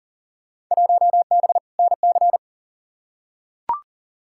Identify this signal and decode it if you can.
Morse code, heard as a keyed tone, about 37 wpm, 700 Hz. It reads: 1B NC
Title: Field Day Exchanges - 40wpm